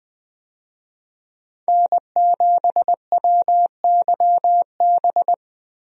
Morse code, heard as a keyed tone, about 20 wpm, 700 Hz. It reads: N7WYB
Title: USA Call Signs - 5 Characters - Repeated 3x - 20wpm